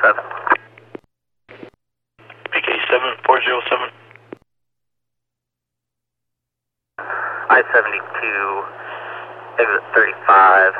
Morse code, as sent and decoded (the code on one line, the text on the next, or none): none